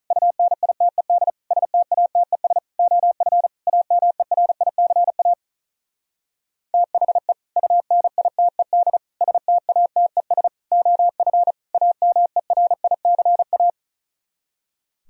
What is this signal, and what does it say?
Morse code, heard as a keyed tone, about 43 wpm, 700 Hz. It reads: UNITED STATES OF AMERICA THE UNITED STATES OF AMERICA
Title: Speed-Racing - Sets of 5 Words 35wpm